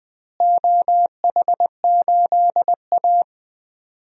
Morse code, heard as a keyed tone, about 20 wpm, 700 Hz. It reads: OH8A
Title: DXCC-100 Call Signs - 4 Characters - Repeated 2x - 20wpm